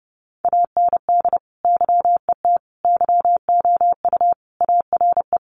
Morse code, heard as a keyed tone, about 30 wpm, 700 Hz. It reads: AND YET YOU ARE
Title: Sentences from Top 500 Words 30wpm